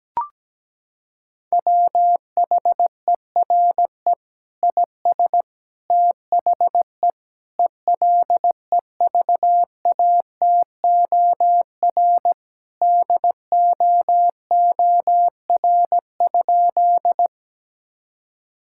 Morse code, heard as a keyed tone, about 17 wpm, 700 Hz. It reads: WHERE IS THE ELEVATOR DOOR?
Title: Sentences from Top 1600 Words 17wpm